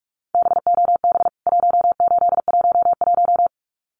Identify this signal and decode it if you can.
Morse code, heard as a keyed tone, about 44 wpm, 700 Hz. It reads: BOB 1811
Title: CWT Exchange 45wpm